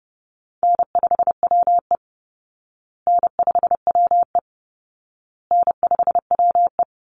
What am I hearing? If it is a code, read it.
Morse code, heard as a keyed tone, about 30 wpm, 700 Hz. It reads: N5WE N5WE N5WE
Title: Call Signs Repeated 2x - 30wpm